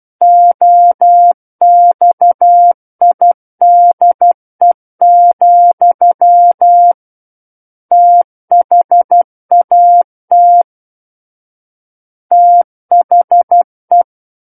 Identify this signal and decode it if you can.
Morse code, heard as a keyed tone, about 12 wpm, 700 Hz. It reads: OXIDE, THAT THE